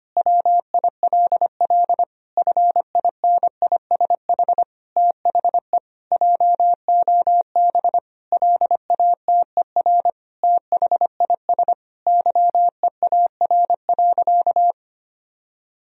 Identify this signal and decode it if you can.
Morse code, heard as a keyed tone, about 25 wpm, 700 Hz. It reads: WILL FINISH THE JOB LATER THIS YEAR.
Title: Sentences from Top 1000 Words - Review Part 3 - 25wpm